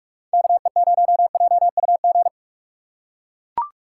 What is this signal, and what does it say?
Morse code, heard as a keyed tone, about 42 wpm, 700 Hz. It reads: KE0JUG
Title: USA Call Signs - 6 Characters - 6x Character Spacing - 45wpm